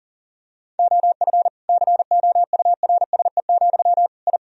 Morse code, heard as a keyed tone, about 37 wpm, 700 Hz. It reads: OF COURSE, I
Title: Sentences from Top 1000 Words - Review Part 4 - 40wpm